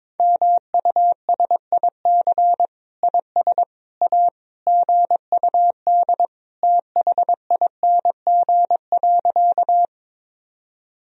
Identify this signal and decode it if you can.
Morse code, heard as a keyed tone, about 21 wpm, 700 Hz. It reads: MUSIC IS A GUD THING.
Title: Sentences from Top 300 Words and Abbreviations 22wpm